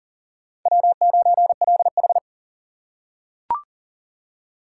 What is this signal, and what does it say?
Morse code, heard as a keyed tone, about 40 wpm, 700 Hz. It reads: W9LH
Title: USA Call Signs - 4 Characters - 6x Character Spacing - 40wpm